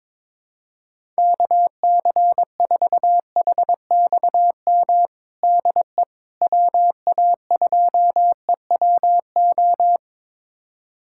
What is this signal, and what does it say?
Morse code, heard as a keyed tone, about 22 wpm, 700 Hz. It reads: KC4HXM DE WA2EWO
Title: USA Callsign DE Callsign 22wpm